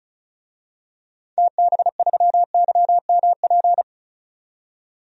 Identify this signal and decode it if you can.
Morse code, heard as a keyed tone, about 36 wpm, 700 Hz. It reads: TB3YMP
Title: DXCC-100 Call Signs - 6 Characters - 35wpm